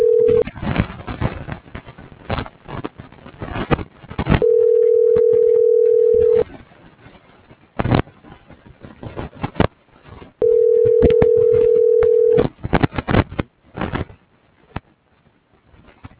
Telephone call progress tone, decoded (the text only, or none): ringback tone